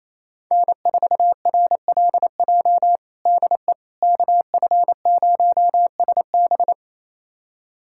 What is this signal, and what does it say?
Morse code, heard as a keyed tone, about 27 wpm, 700 Hz. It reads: N4RLJ DE KF0SB
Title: USA Callsign DE Callsign - 5 Characters - 28wpm